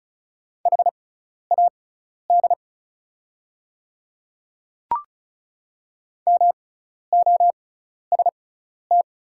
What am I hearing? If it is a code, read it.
Morse code, heard as a keyed tone, about 36 wpm, spaced as for 12 wpm, 690 Hz. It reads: HAD MOST
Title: 6x Character Spacing - Top 500 Words - 35wpm